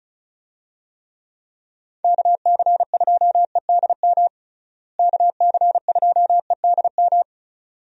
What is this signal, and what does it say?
Morse code, heard as a keyed tone, about 36 wpm, 700 Hz. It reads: KC2EDM KC2EDM
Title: USA Call Signs - 6 Characters - Repeated 2x - 35wpm